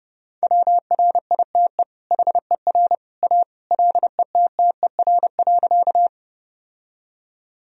Morse code, heard as a keyed tone, about 30 wpm, 700 Hz. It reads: WRITE HER A LETTER.
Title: Sentences from Top 1000 Words - Review Part 1 - 30wpm